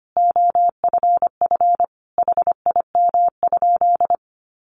Morse code, heard as a keyed tone, about 25 wpm, 700 Hz. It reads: OFF HIM?
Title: Sentences from Top 1300 Words 25wpm